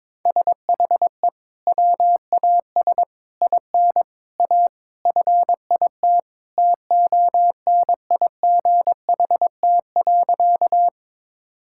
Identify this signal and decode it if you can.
Morse code, heard as a keyed tone, about 22 wpm, 700 Hz. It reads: SHE WAS IN A FIT TONIGHT.
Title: Sentences from Top 1000 Words - Review Part 5 - 22wpm